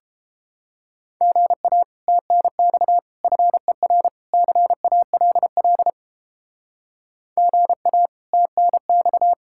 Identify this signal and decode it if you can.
Morse code, heard as a keyed tone, about 32 wpm, 700 Hz. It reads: GA TNX FER CALL GA TNX